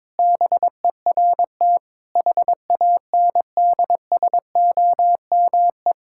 Morse code, heard as a keyed tone, about 22 wpm, 700 Hz. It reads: BERT HANDSOME
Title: Pangram Challenge v2 - 22wpm